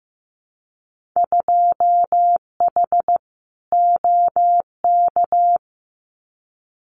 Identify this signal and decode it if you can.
Morse code, heard as a keyed tone, about 15 wpm, 700 Hz. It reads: 2H OK
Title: Winter Field Day Exchanges Encore 15wpm